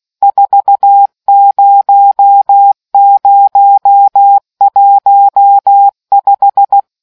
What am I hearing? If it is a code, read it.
Morse code, heard as a keyed tone, about 16 wpm, 800 Hz. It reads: 40015